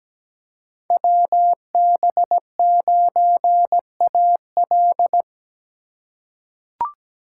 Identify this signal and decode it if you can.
Morse code, heard as a keyed tone, about 17 wpm, 700 Hz. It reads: WB9AL
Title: USA Call Signs - 5 Characters - 8x Character Spacing - 17wpm